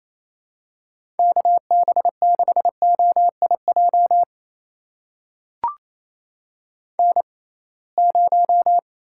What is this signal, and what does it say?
Morse code, heard as a keyed tone, about 28 wpm, spaced as for 21 wpm, 700 Hz. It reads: KB6OIJ N0
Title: USA Call Signs - 6x Character Spacing - 28wpm